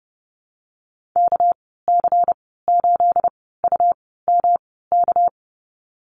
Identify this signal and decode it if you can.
Morse code, heard as a keyed tone, about 30 wpm, spaced as for 17 wpm, 700 Hz. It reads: KC8UMK
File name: Lesson 37g - Letter x - Call Signs 30wpm